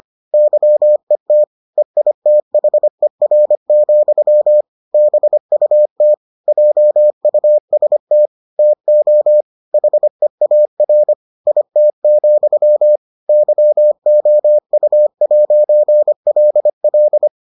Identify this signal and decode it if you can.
Morse code, heard as a keyed tone, about 25 wpm, 600 Hz. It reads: YET EITHER, BUT JUST TO HEAR IT, YOU'LL